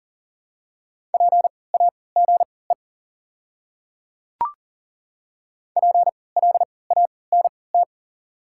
Morse code, heard as a keyed tone, about 40 wpm, spaced as for 23 wpm, 690 Hz. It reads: PAGE PLANT
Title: 3x Character Spacing - Top 500 Words - 40wpm